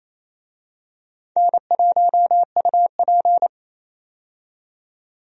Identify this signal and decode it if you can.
Morse code, heard as a keyed tone, about 28 wpm, 700 Hz. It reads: N1UP
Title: USA Call Signs - 4 Characters - 28wpm